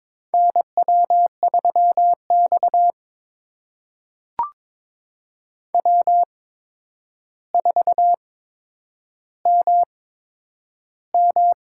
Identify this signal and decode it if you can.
Morse code, heard as a keyed tone, about 21 wpm, spaced as for 9 wpm, 700 Hz. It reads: NW3X W4MM
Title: USA Call Signs - 4 Characters - 8x Character Spacing - 22wpm